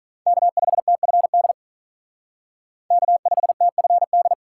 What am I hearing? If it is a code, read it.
Morse code, heard as a keyed tone, about 40 wpm, 700 Hz. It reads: K5TFD K5TFD
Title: Speed Racing - USA 1x3 Call Signs 35wpm